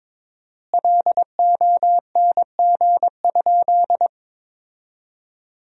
Morse code, heard as a keyed tone, about 22 wpm, 700 Hz. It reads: LONG?
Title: 4x Word-Spacing - Sentences from Top 200 Words - 22wpm